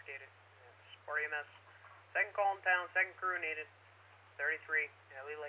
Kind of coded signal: Emergency Alert System (SAME) tones